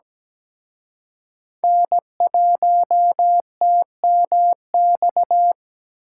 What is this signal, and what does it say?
Morse code, heard as a keyed tone, about 20 wpm, 700 Hz. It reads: N1TMX
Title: Speed Racing - USA 1x3 Call Signs 17wpm